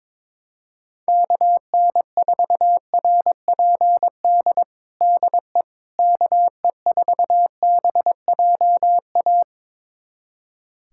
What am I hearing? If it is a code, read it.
Morse code, heard as a keyed tone, about 21 wpm, 700 Hz. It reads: KN4RPD DE KE4BJA
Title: USA Callsign DE Callsign - 6 Characters - 22wpm